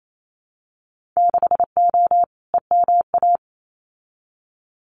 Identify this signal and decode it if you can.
Morse code, heard as a keyed tone, about 28 wpm, 700 Hz. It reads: 6O EMA
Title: Winter Field Day Exchanges Encore 28wpm